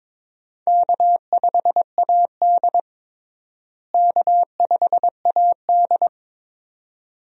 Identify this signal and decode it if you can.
Morse code, heard as a keyed tone, about 22 wpm, 700 Hz. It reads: K5AD K5AD
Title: USA Call Signs - 4 Characters - Repeated 2x - 22wpm